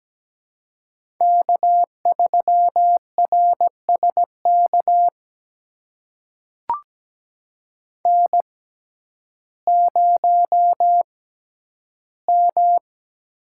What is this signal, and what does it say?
Morse code, heard as a keyed tone, about 17 wpm, spaced as for 10 wpm, 700 Hz. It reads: K3RSK N0M
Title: USA Call Signs - 5 Characters - 6x Character Spacing - 17wpm